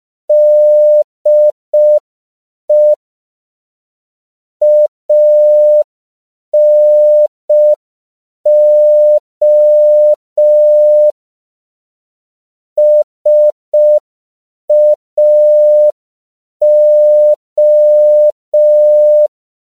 Morse code, heard as a keyed tone, about 5 wpm, 600 Hz. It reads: DE ANO SAO
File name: CW / 5PPM